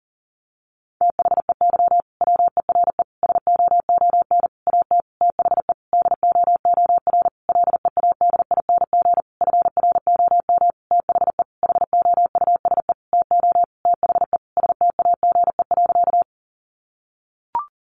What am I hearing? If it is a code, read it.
Morse code, heard as a keyed tone, about 40 wpm, 700 Hz. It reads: THEY WERE SOON AT THE DOOR LEADING FROM THE HOUSE TO THE STAGE.
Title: Sentences from Top 1200 Words 40wpm